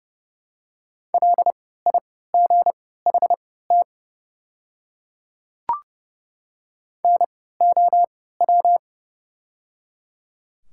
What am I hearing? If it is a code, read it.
Morse code, heard as a keyed tone, about 30 wpm, spaced as for 17 wpm, 700 Hz. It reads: LIGHT NOW
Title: 3x Character Spacing - Top 200 Words - 30wpm